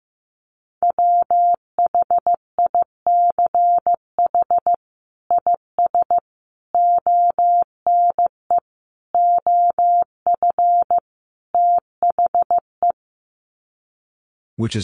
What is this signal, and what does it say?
Morse code, heard as a keyed tone, about 15 wpm, 700 Hz. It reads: WHICH IS ONE OF THE
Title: Sets of 5 Words Encore 15wpm